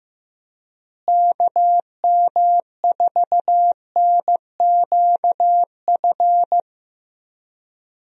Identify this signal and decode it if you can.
Morse code, heard as a keyed tone, about 15 wpm, 700 Hz. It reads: KM4NQF
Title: USA Call Signs - 6 Characters - Repeated 3x - 15wpm